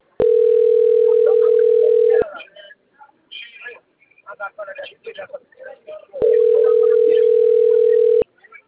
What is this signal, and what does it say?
Ringback tone (North America), 440 + 480 Hz, with speech.
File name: out- - - - - .98284